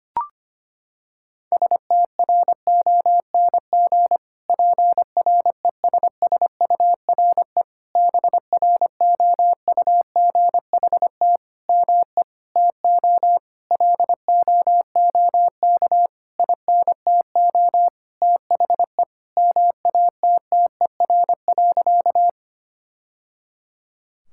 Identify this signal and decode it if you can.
Morse code, heard as a keyed tone, about 25 wpm, 700 Hz. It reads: STRONG PRESSURE BROUGHT ME TO LOOK INTO THE MATTER.
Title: Sentences from Top 1000 Words - Review Part 6 - 25wpm